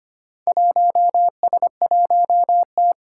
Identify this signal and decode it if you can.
Morse code, heard as a keyed tone, about 25 wpm, 700 Hz. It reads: 1S1T
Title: Four Challenging Letter-Number Combinations 25wpm